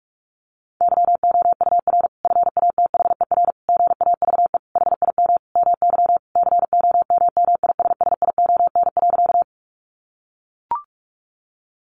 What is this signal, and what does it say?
Morse code, heard as a keyed tone, about 44 wpm, 700 Hz. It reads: YOUR FATHER GAVE HIM MY COMMISSION.
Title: Sentences from Top 1400 Words 45wpm